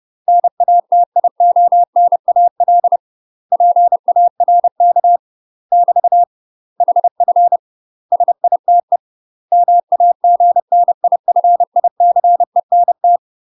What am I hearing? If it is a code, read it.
Morse code, heard as a keyed tone, about 30 wpm, 700 Hz. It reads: NATIONAL PARK <BT> HF SITE MAGNIFICENT
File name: UK MASTER QSOs TRAINING / 30WPM / UK MASTER QSOs 0024